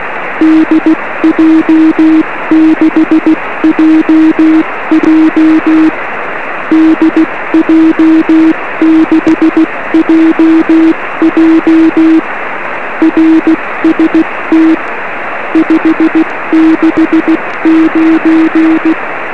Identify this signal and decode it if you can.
Morse code, heard as a keyed tone, about 16 wpm, 320 Hz, with noise. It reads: DJ6JJ DJ6JJ RST 569
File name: 1911z dj6ff 13cm